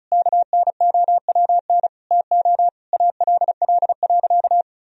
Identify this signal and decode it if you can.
Morse code, heard as a keyed tone, about 34 wpm, 700 Hz. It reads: KNOWN TO ALL.
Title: Sentences from Top 1000 Words - Review Part 4 - 35wpm